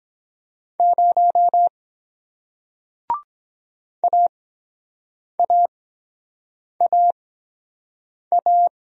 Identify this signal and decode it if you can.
Morse code, heard as a keyed tone, about 23 wpm, spaced as for 7 wpm, 700 Hz. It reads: 0 AAAA